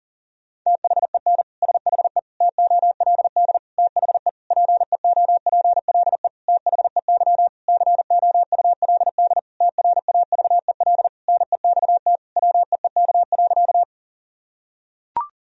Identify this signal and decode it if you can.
Morse code, heard as a keyed tone, about 40 wpm, 700 Hz. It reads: THEN SHE TOLD THE PEOPLE THEY COULD TRAVEL NEXT WEEK.
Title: Sentences from Top 1000 Words - Review Part 2 - 40wpm